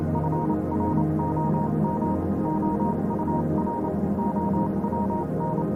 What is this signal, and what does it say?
Morse code, heard as a keyed tone, about 28 wpm, spaced as for 25 wpm, 940 Hz, with noise. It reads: FROM OUR OWN
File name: Vylet Pony / aura / do you remember the song she sang (morse code version)